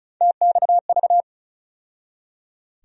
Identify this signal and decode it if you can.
Morse code, heard as a keyed tone, about 34 wpm, 700 Hz. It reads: TXV
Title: USA Call Sign Postfixes - 3 Letters 35wpm